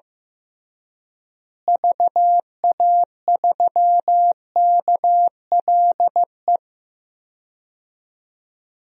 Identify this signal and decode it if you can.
Morse code, heard as a keyed tone, about 15 wpm, 700 Hz. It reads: VA3KLE
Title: DXCC-100 Call Signs - 6 Characters - Repeated 3x - 15wpm